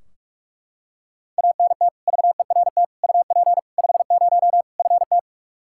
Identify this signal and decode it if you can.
Morse code, heard as a keyed tone, about 41 wpm, 700 Hz. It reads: ANT VERT UP 50 FT
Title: Practice QSO - Exchange 2 - Info - 45wpm